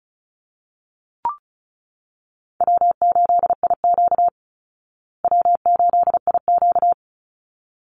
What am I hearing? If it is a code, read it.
Morse code, heard as a keyed tone, about 34 wpm, 700 Hz. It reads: W8IQ W8IQ
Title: Easy Call Signs Repeated 2x - 35wpm